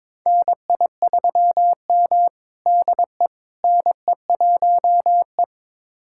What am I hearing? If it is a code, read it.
Morse code, heard as a keyed tone, about 22 wpm, 700 Hz. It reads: NI3M DE NE1E
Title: USA Callsign DE Callsign - 4 Characters - 22wpm